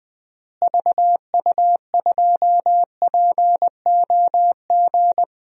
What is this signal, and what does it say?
Morse code, heard as a keyed tone, about 20 wpm, 700 Hz. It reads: VU2POG
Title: DXCC-100 Call Signs - 6 Characters - Repeated 3x - 20wpm